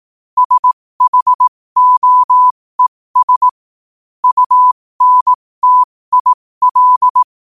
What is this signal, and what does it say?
Morse code, heard as a keyed tone, about 18 wpm, spaced as for 15 wpm, 1000 Hz. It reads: SHOES UNTIL
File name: qotdfiles / 15wpm